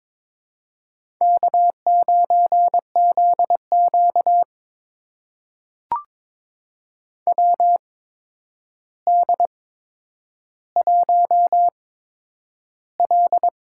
Morse code, heard as a keyed tone, about 23 wpm, spaced as for 10 wpm, 700 Hz. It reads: K9ZQ WD1L
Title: USA Call Signs - 4 Characters - 8x Character Spacing - 22wpm